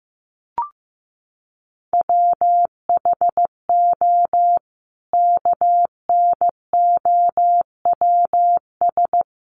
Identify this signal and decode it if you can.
Morse code, heard as a keyed tone, about 15 wpm, 700 Hz. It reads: WHO KNOWS
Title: A Sound Of Thunder 15wpm